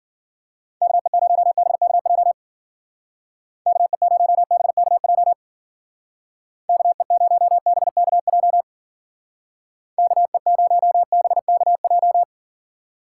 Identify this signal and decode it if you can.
Morse code, heard as a keyed tone, about 50 wpm, 700 Hz. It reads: KE0BKJ KE0BKJ KE0BKJ KE0BKJ
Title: Speed Racing - USA 2x3 Call Signs 40wpm